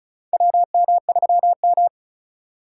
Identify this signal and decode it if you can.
Morse code, heard as a keyed tone, about 36 wpm, 700 Hz. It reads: WM3M
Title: USA Call Signs - 2x Character Spacing - 35wpm